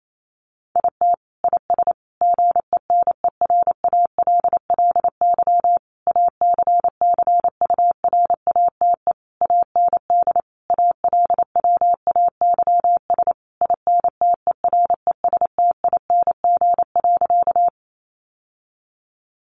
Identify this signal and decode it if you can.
Morse code, heard as a keyed tone, about 28 wpm, 700 Hz. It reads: IT IS GENERALLY ACCURATE AND ALWAYS INTERESTING.
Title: Sentences from Top 1200 Words 28wpm